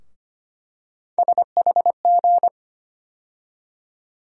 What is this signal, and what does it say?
Morse code, heard as a keyed tone, about 25 wpm, 700 Hz. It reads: SHG
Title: USA Call Sign Postfixes - 3 Letters 25wpm